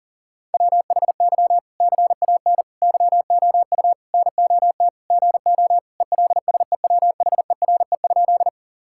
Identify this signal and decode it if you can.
Morse code, heard as a keyed tone, about 37 wpm, 700 Hz. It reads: WHY CAN YOU NOT GO ELSEWHERE?
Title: Sentences from Top 1600 Words 40wpm